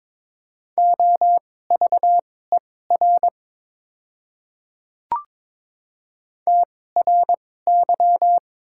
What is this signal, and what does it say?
Morse code, heard as a keyed tone, about 22 wpm, spaced as for 16 wpm, 700 Hz. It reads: OVER TRY
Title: 2x Character Spacing - Top 200 Words - 22wpm